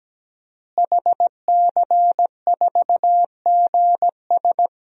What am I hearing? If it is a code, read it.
Morse code, heard as a keyed tone, about 17 wpm, 700 Hz. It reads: HC4GS
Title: DXCC-100 Call Signs - 5 Characters - Repeated 3x - 17wpm